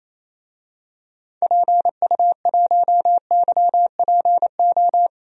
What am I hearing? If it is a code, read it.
Morse code, heard as a keyed tone, about 28 wpm, 700 Hz. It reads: PU1YPO
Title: DXCC-100 Call Signs - 6 Characters - 28wpm